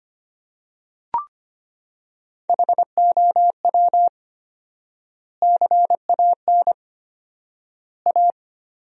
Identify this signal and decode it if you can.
Morse code, heard as a keyed tone, about 25 wpm, 700 Hz. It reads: HOW CAN A
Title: 4x Word-Spacing - Sentences from Top 500 Words - 25wpm